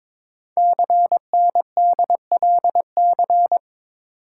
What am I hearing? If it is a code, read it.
Morse code, heard as a keyed tone, about 23 wpm, 700 Hz. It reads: CNDLC
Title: Code Groups - 5 Letters - 22wpm